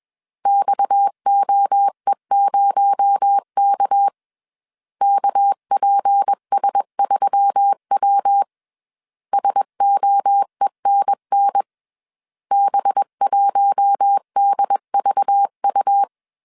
Morse code, heard as a keyed tone, about 21 wpm, 800 Hz. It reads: XOE0K KPS3W SOENN B1DVU
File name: CW / DET / 1 6 1500C S B 20wpm 2048